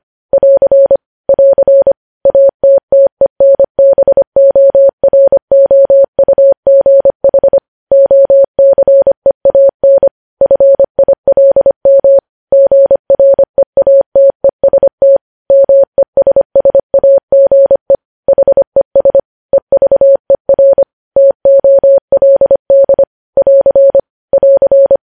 Morse code, heard as a keyed tone, about 25 wpm, 550 Hz. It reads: <AR> <AR> ATTENBOROUGH OCEAN FILM GREATEST MESSAGE HES EVER TOLD <AR> <AR>
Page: news headlines in morse code at 25 wpm